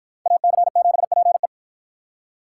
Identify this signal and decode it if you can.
Morse code, heard as a keyed tone, about 51 wpm, 700 Hz. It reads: AC7PE